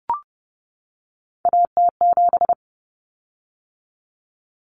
Morse code, heard as a keyed tone, about 30 wpm, 700 Hz. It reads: AT7
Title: Common Cut Numbers - 3-Digit 30wpm